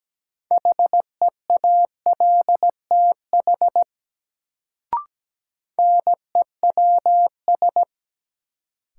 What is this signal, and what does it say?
Morse code, heard as a keyed tone, about 17 wpm, 700 Hz. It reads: HEALTH NEWS
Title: Sets of 5 Words - New Word Drill - 17wpm